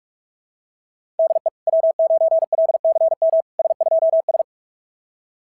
Morse code, heard as a keyed tone, about 44 wpm, 660 Hz. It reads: DE W9LCM S2S
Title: SOTA QSO Practice 45wpm